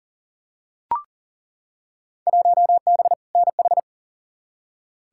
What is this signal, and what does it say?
Morse code, heard as a keyed tone, about 40 wpm, 700 Hz. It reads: 1B NH
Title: Field Day Exchanges Encore - 40wpm